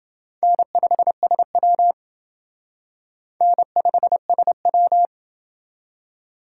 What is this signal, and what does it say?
Morse code, heard as a keyed tone, about 28 wpm, 700 Hz. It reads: N5SW N5SW